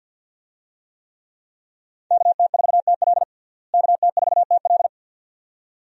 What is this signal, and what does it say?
Morse code, heard as a keyed tone, about 49 wpm, 700 Hz. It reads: KT4TL KT4TL
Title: USA Call Signs - 5 Characters - Repeated 2x - 50wpm